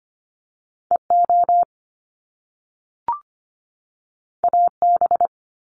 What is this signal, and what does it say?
Morse code, heard as a keyed tone, about 25 wpm, 700 Hz. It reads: EO AB
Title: Top Two Letter Combinations in Words 25wpm